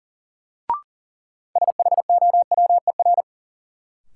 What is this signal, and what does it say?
Morse code, heard as a keyed tone, about 40 wpm, 700 Hz. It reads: SHOWER
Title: Sentences from Top 1600 Words - New Word Drill Alt - 40wpm